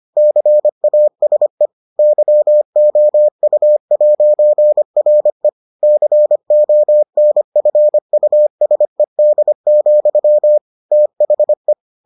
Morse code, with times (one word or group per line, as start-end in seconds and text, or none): 0.2-1.7 CASE
2.0-5.5 YOU'RE
5.8-10.6 CONFUSED,
10.9-11.7 THE